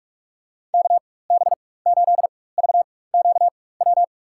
Morse code, heard as a keyed tone, about 48 wpm, spaced as for 21 wpm, 690 Hz. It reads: KB8VQW
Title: USA Call Signs - 4x Character Spacing - 45wpm